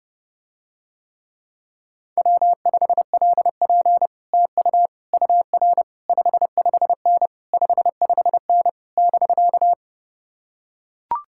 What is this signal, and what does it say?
Morse code, heard as a keyed tone, about 30 wpm, 700 Hz. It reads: W5LP TU UR 55N 55N <BK>